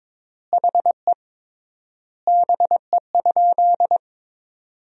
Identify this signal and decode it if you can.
Morse code, heard as a keyed tone, about 22 wpm, 700 Hz. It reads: HE BE?
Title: 3x Word-Spacing - Sentences from Top 200 Words - 22wpm